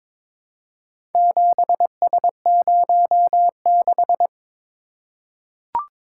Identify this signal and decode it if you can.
Morse code, heard as a keyed tone, about 22 wpm, 700 Hz. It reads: 7S06
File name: practice / 22wpm / Four Letter-Number Combinations - 22wpm